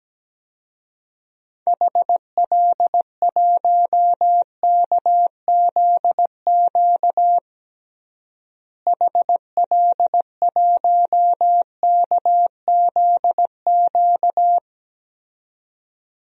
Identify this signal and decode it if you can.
Morse code, heard as a keyed tone, about 17 wpm, 700 Hz. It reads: HL1KZQ HL1KZQ
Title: DXCC-100 Call Signs - 6 Characters - Repeated 2x - 17wpm